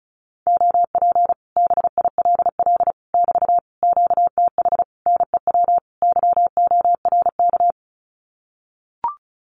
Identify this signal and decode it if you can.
Morse code, heard as a keyed tone, about 34 wpm, 700 Hz. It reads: OP BILL <BT> QTH NEW YORK
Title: Common QSO Phrases Encore 35wpm